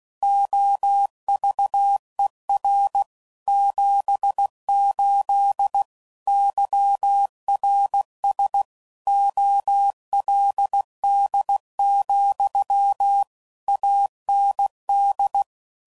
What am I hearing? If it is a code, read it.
Morse code, heard as a keyed tone, about 16 wpm, 800 Hz. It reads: OVER 78 YRS OLD, AND
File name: CW Code Groups @15WPM / QSOs / WA7QOX QRX